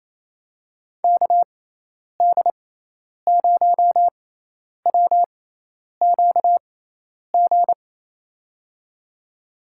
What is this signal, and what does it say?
Morse code, heard as a keyed tone, about 27 wpm, spaced as for 10 wpm, 700 Hz. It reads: KD0WQG
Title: USA Call Signs - 6x Character Spacing - 28wpm